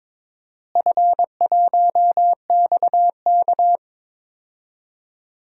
Morse code, heard as a keyed tone, about 22 wpm, 700 Hz. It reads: F1XK
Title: DXCC-100 Call Signs - 4 Characters - Repeated 2x - 22wpm